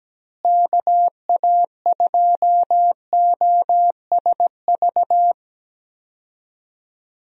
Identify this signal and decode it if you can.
Morse code, heard as a keyed tone, about 17 wpm, 700 Hz. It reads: KA2OSV
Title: USA Call Signs - 6 Characters - Repeated 2x - 17wpm